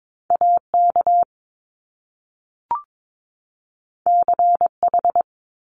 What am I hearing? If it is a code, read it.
Morse code, heard as a keyed tone, about 22 wpm, 700 Hz. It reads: AK CH
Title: Top Two Letter Combinations in Words 22wpm